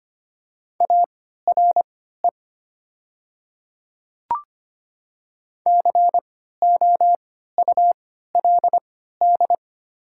Morse code, heard as a keyed tone, about 25 wpm, spaced as for 14 wpm, 700 Hz. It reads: ARE COULD
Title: 3x Character Spacing - Top 100 Words - 25wpm